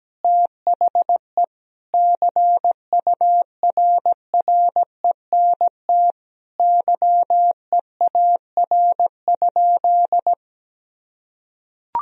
Morse code, heard as a keyed tone, about 17 wpm, 700 Hz. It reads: THE CURRENT YEAR?
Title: Sentences from Top 1000 Words - Review Part 4 - 17wpm